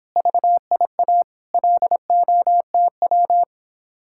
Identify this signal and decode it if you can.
Morse code, heard as a keyed tone, about 31 wpm, 700 Hz. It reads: VIA LOTW